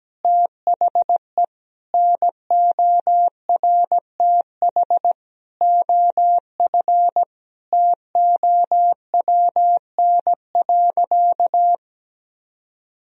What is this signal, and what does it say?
Morse code, heard as a keyed tone, about 17 wpm, 700 Hz. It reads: THE NORTH OF TOWN.
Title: Sentences from Top 1000 Words - Review Part 4 - 17wpm